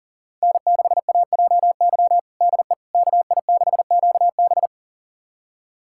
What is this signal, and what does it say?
Morse code, heard as a keyed tone, about 37 wpm, 700 Hz. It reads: N6AJY DE KI6QB
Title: USA Callsign DE Callsign - 5 Characters - 40wpm